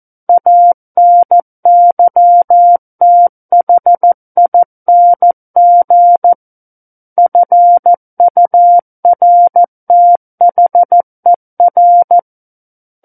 Morse code, heard as a keyed tone, about 14 wpm, 700 Hz. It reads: ANYTHING FURTHER
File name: Book One / 05-Chapter 5 - 14 WPM